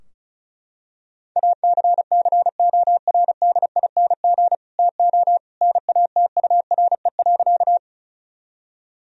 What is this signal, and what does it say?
Morse code, heard as a keyed tone, about 34 wpm, 700 Hz. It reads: ACCORDING TO NATURE.
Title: Sentences from Top 1000 Words - Review Part 4 - 35wpm